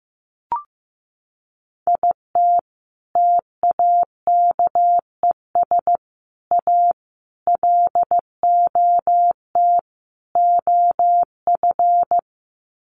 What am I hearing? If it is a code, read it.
Morse code, heard as a keyed tone, about 15 wpm, 700 Hz. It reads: IT TAKES A LOT OF
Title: Sets of 5 Words Encore 15wpm